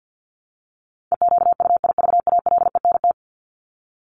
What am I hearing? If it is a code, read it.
Morse code, heard as a keyed tone, about 50 wpm, 700 Hz. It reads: EQUIVALENT
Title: Top 1501-1750 Words 50wpm